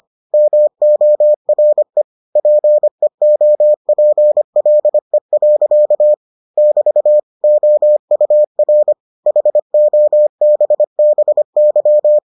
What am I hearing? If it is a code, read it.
Morse code, heard as a keyed tone, about 25 wpm, 600 Hz. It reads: MORE PEOPLE. <BT> OUR HOBBY